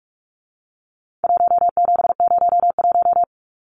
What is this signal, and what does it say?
Morse code, heard as a keyed tone, about 44 wpm, 700 Hz. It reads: 1701
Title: Numbers - 4-Digit 45wpm